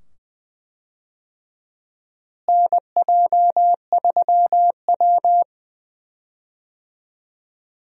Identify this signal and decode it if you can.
Morse code, heard as a keyed tone, about 21 wpm, 700 Hz. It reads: NJ3W
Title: USA Call Signs - 4 Characters - Repeated 2x - 20wpm